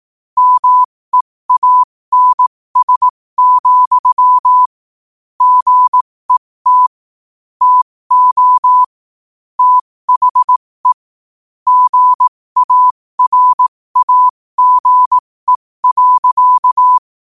Morse code, heard as a keyed tone, about 18 wpm, spaced as for 15 wpm, 1000 Hz. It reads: MEANS, GET TO THE GARAGE.
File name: qotdfiles / 15wpm